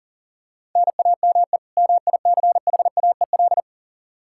Morse code, heard as a keyed tone, about 40 wpm, 700 Hz. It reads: NAME MICHAEL
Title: Practice QSO - Exchange 1 - Beginning - 40wpm